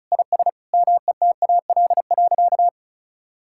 Morse code, heard as a keyed tone, about 36 wpm, 700 Hz. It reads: IS METAL.
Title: Sentences from Top 1600 Words 35wpm